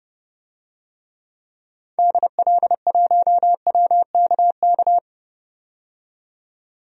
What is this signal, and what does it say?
Morse code, heard as a keyed tone, about 30 wpm, 690 Hz. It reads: DL1WKK
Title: DXCC-100 Call Signs - 6 Characters - 30wpm